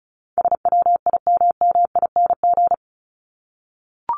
Morse code, heard as a keyed tone, about 34 wpm, 700 Hz. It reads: SWIMMING
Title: Top 1501-1750 Words 35wpm